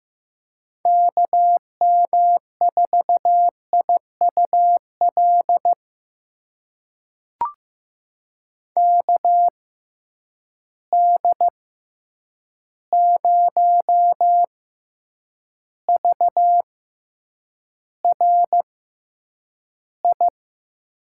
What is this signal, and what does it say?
Morse code, heard as a keyed tone, about 15 wpm, spaced as for 8 wpm, 700 Hz. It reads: KM4IUL KD0VRI
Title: USA Call Signs - 6 Characters - 6x Character Spacing - 15wpm